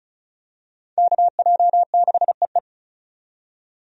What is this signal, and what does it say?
Morse code, heard as a keyed tone, about 36 wpm, 700 Hz. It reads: KJ6EE
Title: USA Call Signs - 5 Characters - 35wpm